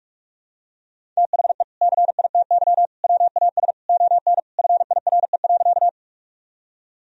Morse code, heard as a keyed tone, about 46 wpm, 700 Hz. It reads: THE CITY WAS ON FIRE.
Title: Sentences from Top 1000 Words - Review Part 1 - 45wpm